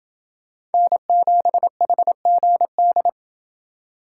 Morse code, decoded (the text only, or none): N7HGD